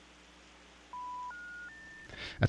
Telephone call progress tone, heard as a telephone network's tone, with noise. Special information tone.